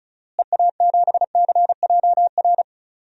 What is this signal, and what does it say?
Morse code, heard as a keyed tone, about 36 wpm, 700 Hz. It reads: EA7CJR
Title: DXCC-100 Call Signs - 6 Characters - Repeated 3x - 35wpm